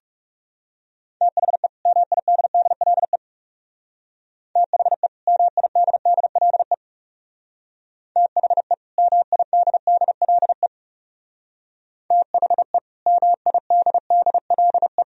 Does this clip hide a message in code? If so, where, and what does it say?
Yes: Morse code from 1.2 to 15.0 s, heard as a keyed tone, about 40 wpm, 700 Hz. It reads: THE MIDDLE THE MIDDLE THE MIDDLE THE MIDDLE